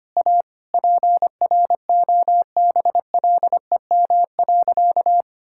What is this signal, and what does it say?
Morse code, heard as a keyed tone, about 25 wpm, 700 Hz. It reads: A PROBLEM.
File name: practice / 25wpm / Sentences with 4-Word Phrases - 25wpm